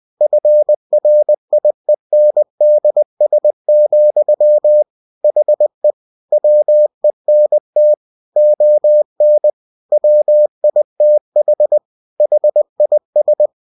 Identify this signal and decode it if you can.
Morse code, heard as a keyed tone, about 20 wpm, 600 Hz. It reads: FRIENDS, HE WENT ON WITH HIS
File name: cw-books / winnie-the-pooh / winnie-20wpm-page-0020